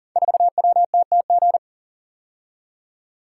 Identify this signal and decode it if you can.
Morse code, heard as a keyed tone, about 41 wpm, 700 Hz. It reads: 4WTTG
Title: Code Groups - 5 Letter:Numbers - 40wpm